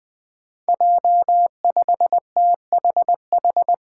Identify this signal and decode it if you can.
Morse code, heard as a keyed tone, about 21 wpm, 700 Hz. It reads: J5THH
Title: Five Challenging Letter-Number Combinations 20wpm